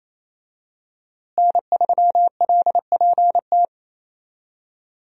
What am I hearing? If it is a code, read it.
Morse code, heard as a keyed tone, about 28 wpm, 700 Hz. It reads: N3LPT
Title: USA Call Signs - 5 Characters - Repeated 2x - 28wpm